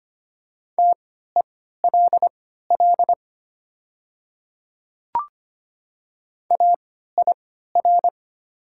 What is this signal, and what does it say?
Morse code, heard as a keyed tone, about 25 wpm, spaced as for 14 wpm, 690 Hz. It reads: TELL AIR
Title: 3x Character Spacing - Top 200 Words - 25wpm